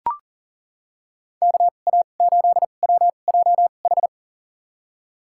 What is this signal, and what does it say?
Morse code, heard as a keyed tone, about 40 wpm, spaced as for 29 wpm, 700 Hz. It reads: KA8WJH
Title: USA Call Signs - 6 Characters - 2x Character Spacing - 40wpm